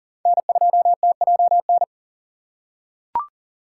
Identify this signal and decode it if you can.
Morse code, heard as a keyed tone, about 40 wpm, 700 Hz. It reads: N2TJN
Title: USA Call Signs - 3x Character Spacing - 40wpm